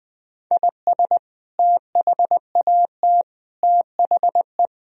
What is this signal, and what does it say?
Morse code, heard as a keyed tone, about 21 wpm, 700 Hz. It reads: IS THAT THE
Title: Sentences from Top 1000 Words - Review Part 2 - 20wpm